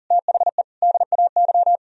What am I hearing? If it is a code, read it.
Morse code, heard as a keyed tone, about 52 wpm, 700 Hz. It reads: THE DAY